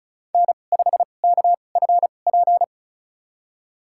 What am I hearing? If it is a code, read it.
Morse code, heard as a keyed tone, about 34 wpm, spaced as for 25 wpm, 700 Hz. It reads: N5KFP
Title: USA Call Signs - 2x Character Spacing - 35wpm